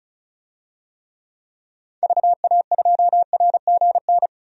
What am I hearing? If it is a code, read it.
Morse code, heard as a keyed tone, about 36 wpm, 700 Hz. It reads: VA2RGN
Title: DXCC-100 Call Signs - 6 Characters - 35wpm